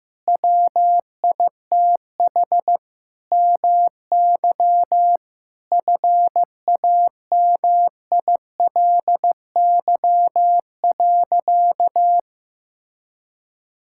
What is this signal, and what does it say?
Morse code, heard as a keyed tone, about 15 wpm, 700 Hz. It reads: WITH MY FAMILY.
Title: Sentences from Top 1600 Words 15wpm